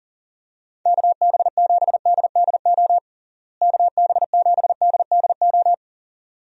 Morse code, heard as a keyed tone, about 40 wpm, 700 Hz. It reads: KB7DDO KB7DDO
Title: USA Call Signs - 6 Characters - Repeated 2x - 40wpm